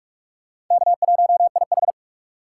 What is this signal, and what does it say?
Morse code, heard as a keyed tone, about 38 wpm, 700 Hz. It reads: K1IH